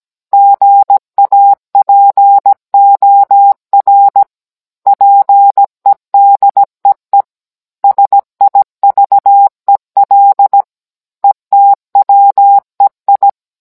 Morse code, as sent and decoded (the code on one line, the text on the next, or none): --. .- .--. --- .-. / .--. -.. . / ... .. ...- . .-.. / . - .-- . ..
GAPOR PEDEE SIVEL ETWEI